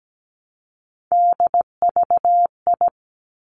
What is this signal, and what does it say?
Morse code, heard as a keyed tone, about 17 wpm, 700 Hz. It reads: DVI
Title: Three Challenging Letter-Number Combinations 17wpm